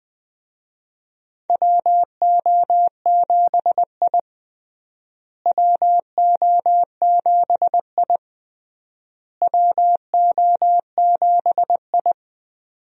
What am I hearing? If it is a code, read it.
Morse code, heard as a keyed tone, about 20 wpm, 700 Hz. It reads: WO7I WO7I WO7I
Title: USA Call Signs - 4 Characters - Repeated 3x - 20wpm